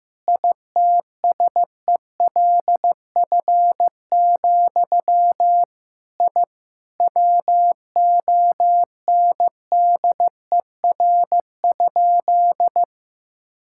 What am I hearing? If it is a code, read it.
Morse code, heard as a keyed tone, about 16 wpm, 700 Hz. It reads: ITSELF, I WONDER?
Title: Sentences from Top 1000 Words - Review Part 5 - 15wpm